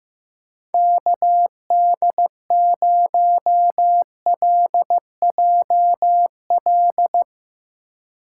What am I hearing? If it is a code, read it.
Morse code, heard as a keyed tone, about 16 wpm, 700 Hz. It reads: KD0LJL
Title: USA Call Signs - 6 Characters - 4x Character Spacing - 15wpm